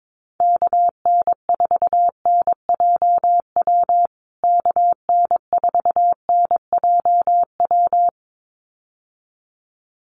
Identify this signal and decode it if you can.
Morse code, heard as a keyed tone, about 22 wpm, 700 Hz. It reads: KN4NJW KN4NJW
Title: USA Call Signs Repeated 2x 22wpm